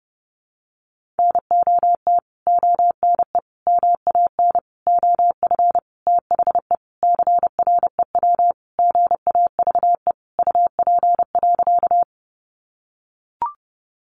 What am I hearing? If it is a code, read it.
Morse code, heard as a keyed tone, about 30 wpm, 700 Hz. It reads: NOT ONE MAN OF THE CREW GAVE UP.
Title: Sentences from Top 1500 Words 30wpm